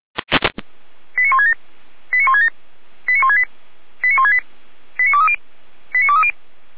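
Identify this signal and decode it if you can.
Five-tone selective call (ZVEI), heard as quick runs of five tones, tones of about 80 ms. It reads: 89167 89167 89168 89168 89230 89230
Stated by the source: Five-tone ZVEI selective call received by SDR, recorded by feuerwehr-orlamuende fire brigade.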